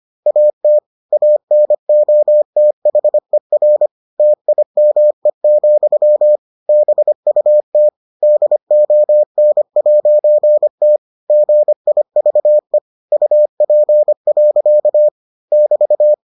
Morse code, heard as a keyed tone, about 25 wpm, 600 Hz. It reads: AT ANOTHER TIME, BUT DON'T GIVE UP. <BT>